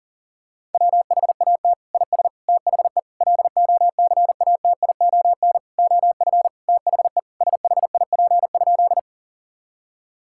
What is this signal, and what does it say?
Morse code, heard as a keyed tone, about 37 wpm, 700 Hz. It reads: WHAT IS THE LOCATION OF THE SHIP?
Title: Sentences from Top 1000 Words - Review Part 6 - 40wpm